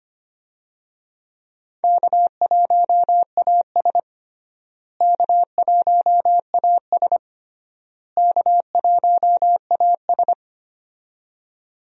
Morse code, heard as a keyed tone, about 25 wpm, 700 Hz. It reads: K1AS K1AS K1AS
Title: USA Call Signs - 4 Characters - Repeated 3x - 25wpm